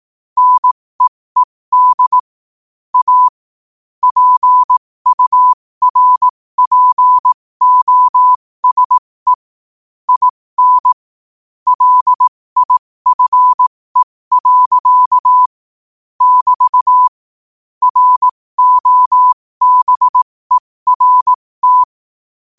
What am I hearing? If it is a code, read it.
Morse code, heard as a keyed tone, about 18 wpm, spaced as for 15 wpm, 1000 Hz. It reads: NEED A PURPOSE IN LIFE. <BT> ROBERT
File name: qotdfiles / 15wpm